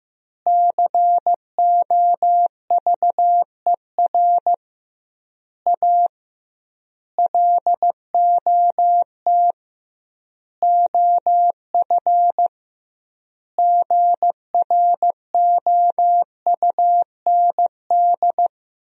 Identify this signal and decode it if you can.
Morse code, heard as a keyed tone, about 15 wpm, 700 Hz. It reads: COVER A LOT OF GROUND
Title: 2x Word-Spacing - Sentences from Top 500 Words - 15wpm